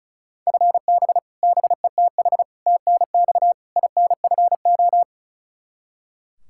Morse code, heard as a keyed tone, about 36 wpm, 700 Hz. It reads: FB BETH TNX INFO
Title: Practice QSO - Exchange 3 - More Info - 35wpm